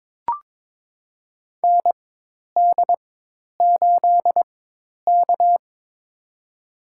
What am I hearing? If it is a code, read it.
Morse code, heard as a keyed tone, about 22 wpm, spaced as for 10 wpm, 700 Hz. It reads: ND8K
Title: USA Call Signs - 4 Characters - 4x Character Spacing - 22wpm